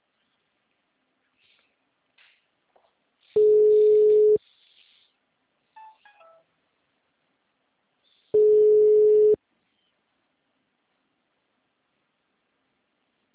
Ringback tone (Europe, 425 Hz).